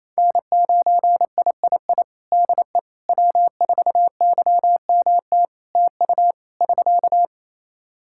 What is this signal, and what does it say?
Morse code, heard as a keyed tone, about 27 wpm, 700 Hz. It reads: N9III DE W4YMT TU <SK>
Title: Practice QSO - Exchange 4 - Ending - 28wpm